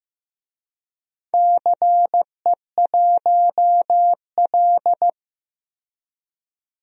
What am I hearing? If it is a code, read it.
Morse code, heard as a keyed tone, about 15 wpm, 700 Hz. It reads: CE1L
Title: DXCC-100 Call Signs - 4 Characters - 15wpm